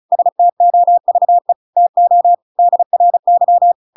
Morse code, heard as a keyed tone, about 34 wpm, 700 Hz. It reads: STOVE TO DRY
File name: Pinocchio / Pinocchio ch7 35 35 ik1hge